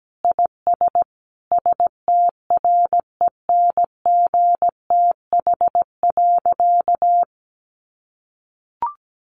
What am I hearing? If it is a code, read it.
Morse code, heard as a keyed tone, about 17 wpm, 700 Hz. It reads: IS STRENGTH.
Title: Sentences from Top 1400 Words 17wpm